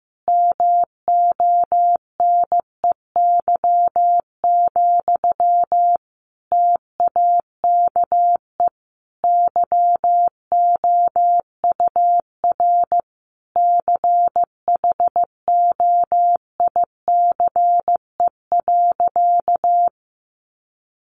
Morse code, heard as a keyed tone, about 15 wpm, 700 Hz. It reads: MONEY, TAKE YOUR CHOICE.
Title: Sentences from Top 900 Words 15wpm